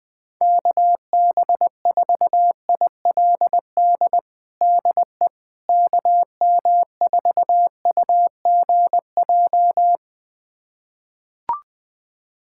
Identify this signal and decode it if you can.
Morse code, heard as a keyed tone, about 20 wpm, 700 Hz. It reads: KB4ILD DE KM4UGJ
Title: USA Callsign DE Callsign - 6 Characters - 20wpm